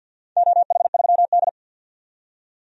Morse code, heard as a keyed tone, about 53 wpm, 700 Hz. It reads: OH3D
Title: DXCC-100 Call Signs - 4 Characters - 50wpm